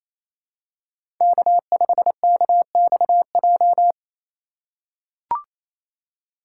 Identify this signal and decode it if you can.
Morse code, heard as a keyed tone, about 28 wpm, 700 Hz. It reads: K5KXJ
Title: K1USN SST QSO Practice - 28wpm